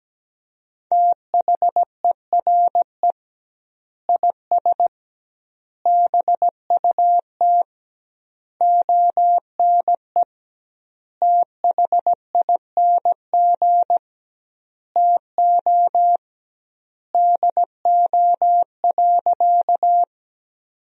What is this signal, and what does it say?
Morse code, heard as a keyed tone, about 17 wpm, 700 Hz. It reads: THERE IS BUT ONE THING TO DO.
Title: 2x Word-Spacing - Sentences from Top 100 Words - 17wpm